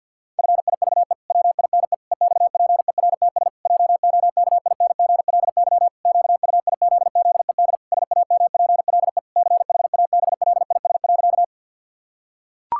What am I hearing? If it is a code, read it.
Morse code, heard as a keyed tone, about 51 wpm, 700 Hz. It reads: FIVE WINE EXPERTS JOKINGLY QUIZZED SAMPLE CHABLIS.
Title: Pangram Challenge v2 - 50wpm